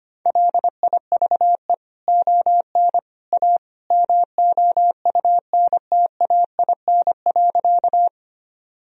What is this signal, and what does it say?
Morse code, heard as a keyed tone, about 25 wpm, 700 Hz. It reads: LIVE ON A MOUNTAIN.
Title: Sentences from Top 1000 Words - Review Part 1 - 25wpm